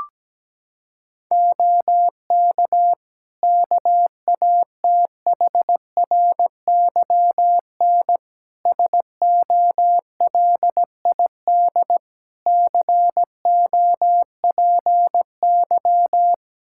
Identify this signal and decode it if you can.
Morse code, heard as a keyed tone, about 17 wpm, 700 Hz. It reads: OK KATHRYN SOLID COPY
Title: K1USN SST QSO Practice 17wpm